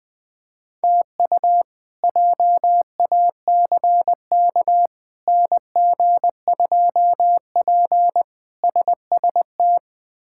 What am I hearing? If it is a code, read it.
Morse code, heard as a keyed tone, about 20 wpm, 700 Hz. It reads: TU JACK NG2P SST
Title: K1USN SST QSO Practice - 20wpm